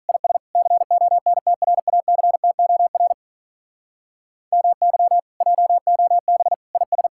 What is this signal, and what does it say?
Morse code, heard as a keyed tone, about 46 wpm, 700 Hz. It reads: IS CONTRACTOR MY JOB IS